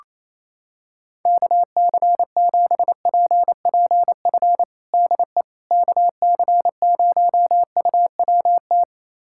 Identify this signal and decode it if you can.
Morse code, heard as a keyed tone, about 27 wpm, 700 Hz. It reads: KC7PPF DE KC0UWT
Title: USA Callsign DE Callsign - 6 Characters - 28wpm